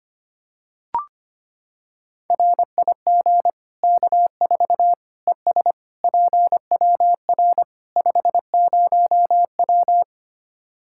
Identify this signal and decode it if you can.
Morse code, heard as a keyed tone, about 25 wpm, 700 Hz. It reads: RIG K4 ES PWR 50W
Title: Practice QSO - Exchange 2 - Info - 25wpm